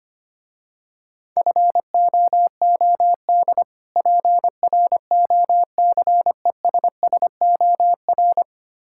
Morse code, heard as a keyed tone, about 26 wpm, 700 Hz. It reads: FOOD PROCESSOR
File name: practice / 25wpm / Difficult Words - Speed-Racing with Hints - 25wpm